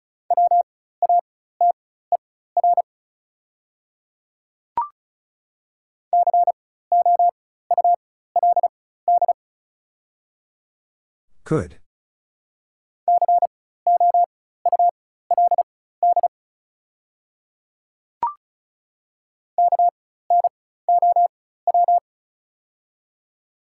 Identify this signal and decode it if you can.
Morse code, heard as a keyed tone, about 36 wpm, spaced as for 16 wpm, 700 Hz. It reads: WATER COULD COULD KNOW
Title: 4x Character Spacing - Top 100 Words - 35wpm